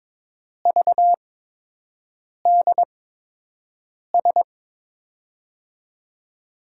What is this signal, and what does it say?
Morse code, heard as a keyed tone, about 22 wpm, spaced as for 6 wpm, 700 Hz. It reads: VDS
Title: USA Call Signs - 6 Characters - 8x Character Spacing - 22wpm